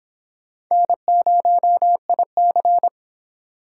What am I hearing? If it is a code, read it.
Morse code, heard as a keyed tone, about 26 wpm, 700 Hz. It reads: N0IC